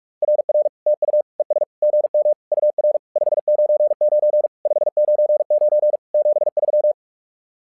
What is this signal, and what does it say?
Morse code, heard as a keyed tone, about 45 wpm, 600 Hz. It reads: RR TU ES GM UR 599 599 73
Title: SOTA QSO Practice 45wpm